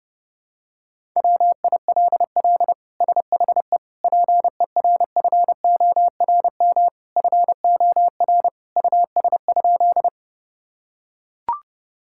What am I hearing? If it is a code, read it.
Morse code, heard as a keyed tone, about 30 wpm, 700 Hz. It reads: WILL SHE PERFORM FOR US?
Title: Sentences from Top 1000 Words - Review Part 5 - 30wpm